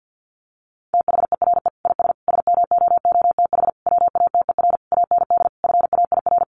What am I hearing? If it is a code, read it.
Morse code, heard as a keyed tone, about 50 wpm, 700 Hz. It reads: THERE IS SMOOTH WATER AND FAIR
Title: Sentences from Top 1400 Words 50wpm